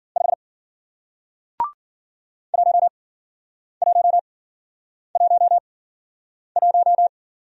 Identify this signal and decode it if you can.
Morse code, heard as a keyed tone, about 51 wpm, spaced as for 9 wpm, 700 Hz. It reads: 5 1111